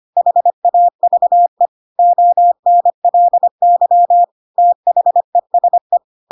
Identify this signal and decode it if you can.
Morse code, heard as a keyed tone, about 25 wpm, 700 Hz. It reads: HAVE ONLY THESE
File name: Pinocchio / Pinocchio ch7 25 25 ik1hge